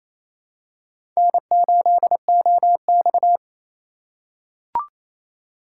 Morse code, heard as a keyed tone, about 27 wpm, 690 Hz. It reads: N8OX
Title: USA Call Signs - 4 Characters - 2x Character Spacing - 28wpm